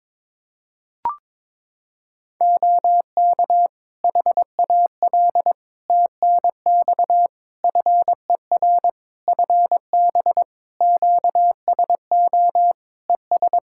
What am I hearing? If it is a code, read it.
Morse code, heard as a keyed tone, about 21 wpm, 700 Hz. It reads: OK HAL TNX FER FB QSO ES
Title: Practice QSO - Exchange 4 - Ending - 22wpm